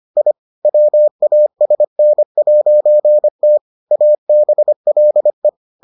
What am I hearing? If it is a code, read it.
Morse code, heard as a keyed tone, about 25 wpm, 600 Hz. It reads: I WASN'T ABLE